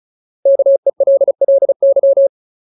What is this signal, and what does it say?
Morse code, heard as a keyed tone, about 36 wpm, 550 Hz. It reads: KELLY
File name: Ultimate Word List / 35WPM Ultimate